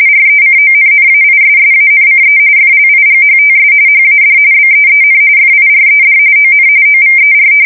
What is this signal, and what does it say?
RTTY, about 45.45 baud. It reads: RECONNAISSANCE FLIGHTS HAD BEEN FIRED ON OVER
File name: fsk170 / battleshipnj